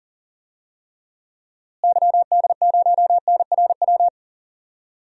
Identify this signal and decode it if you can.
Morse code, heard as a keyed tone, about 39 wpm, 700 Hz. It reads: YD0NRW
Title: DXCC-100 Call Signs - 6 Characters - Repeated 3x - 40wpm